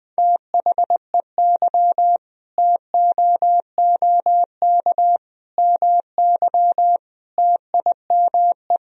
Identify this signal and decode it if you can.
Morse code, heard as a keyed tone, about 20 wpm, 700 Hz. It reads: THEY TOOK MY TIME
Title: Sentences from Top 1000 Words - Review Part 2 - 20wpm